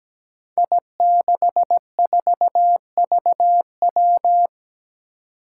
Morse code, heard as a keyed tone, about 17 wpm, 700 Hz. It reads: I64VW
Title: Five Challenging Letter-Number Combinations 17wpm